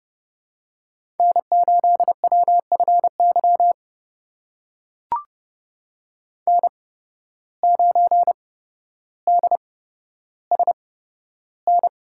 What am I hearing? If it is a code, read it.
Morse code, heard as a keyed tone, about 30 wpm, spaced as for 13 wpm, 700 Hz. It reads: N8WFY N9DSN
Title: USA Call Signs - 5 Characters - 8x Character Spacing - 30wpm